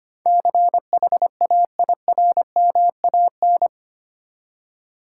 Morse code, heard as a keyed tone, about 25 wpm, 700 Hz. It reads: CHAIRMAN
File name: 8 Letter Words - First 3 Letters Given - 25wpm